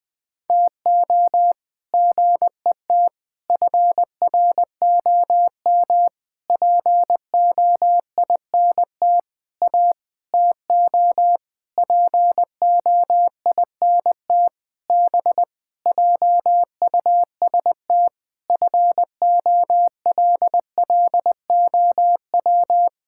Morse code, heard as a keyed tone, about 20 wpm, 700 Hz. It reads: TO GET FROM POINT A TO POINT B JUST FOLLOW
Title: Sentences with 8-Word Phrases - Spoken Beginning - 20wpm